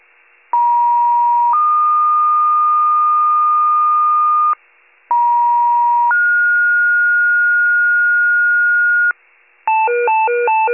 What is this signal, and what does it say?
Two-tone page, heard as a one-second tone, then a three-second tone, about 950 then 1220 Hz, about 950 then 1430 Hz, with noise. It is heard from 0.5 to 9.1 s.